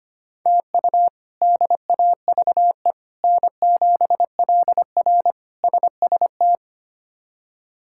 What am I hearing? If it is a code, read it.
Morse code, heard as a keyed tone, about 25 wpm, 700 Hz. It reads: TU DAVE N7LR SST
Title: K1USN SST QSO Practice - 25wpm